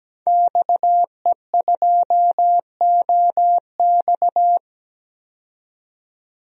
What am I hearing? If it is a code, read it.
Morse code, heard as a keyed tone, about 17 wpm, 700 Hz. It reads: XE2OX
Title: DXCC-100 Call Signs - 5 Characters - 17wpm